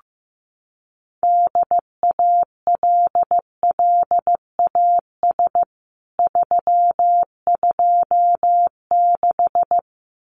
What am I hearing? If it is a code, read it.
Morse code, heard as a keyed tone, about 15 wpm, 700 Hz. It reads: DALLAS 326
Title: CWT Exchange 15wpm